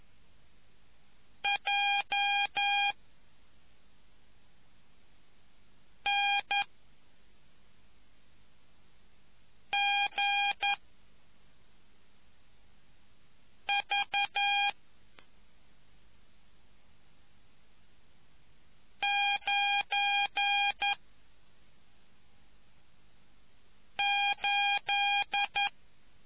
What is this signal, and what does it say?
Morse code, heard as a keyed tone, about 11 wpm, 2450 Hz. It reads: J N G V 9 8